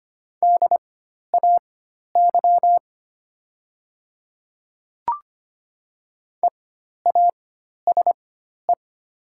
Morse code, heard as a keyed tone, about 25 wpm, spaced as for 12 wpm, 700 Hz. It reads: DAY EASE
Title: 4x Character Spacing - Top 500 Words - 25wpm